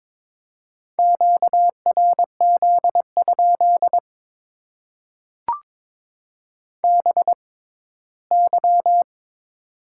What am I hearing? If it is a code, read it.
Morse code, heard as a keyed tone, about 23 wpm, spaced as for 15 wpm, 700 Hz. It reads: QRZ? BY
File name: Top 90 QSO Elements - 6x Character Spacing - 22wpm